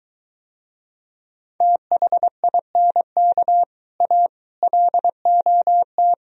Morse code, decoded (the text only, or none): THINK A LOT